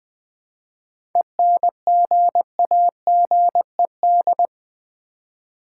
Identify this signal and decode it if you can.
Morse code, heard as a keyed tone, about 20 wpm, 700 Hz. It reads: ENGAGED
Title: 7 Letter Words - First 3 Letters Given - 20wpm